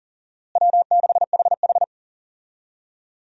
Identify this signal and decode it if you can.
Morse code, heard as a keyed tone, about 37 wpm, 700 Hz. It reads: W6HH
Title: USA Call Signs - 4 Characters - Repeated 2x - 40wpm